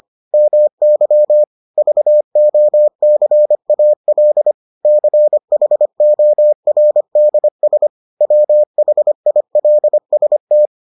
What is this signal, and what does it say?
Morse code, heard as a keyed tone, about 25 wpm, 600 Hz. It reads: MY VOCAL CHORDS WHILST